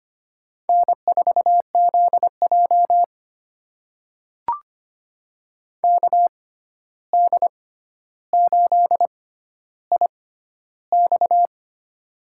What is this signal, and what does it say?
Morse code, heard as a keyed tone, about 25 wpm, spaced as for 12 wpm, 700 Hz. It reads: N4ZJ KD8IX
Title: USA Call Signs - 6x Character Spacing - 25wpm